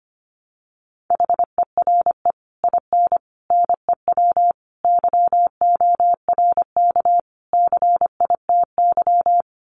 Morse code, heard as a keyed tone, about 25 wpm, 700 Hz. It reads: HERE IN NEW YORK CITY
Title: Sets of 5 Words Encore2 25wpm